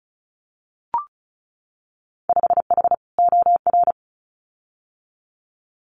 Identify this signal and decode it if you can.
Morse code, heard as a keyed tone, about 36 wpm, 700 Hz. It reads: 5H OR
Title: Winter Field Day Exchanges Encore 35wpm